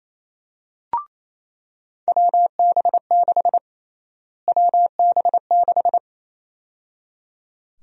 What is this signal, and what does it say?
Morse code, heard as a keyed tone, about 28 wpm, 700 Hz. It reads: WB6 WB6
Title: USA Call Sign Prefixes - Repeated 2x - 28wpm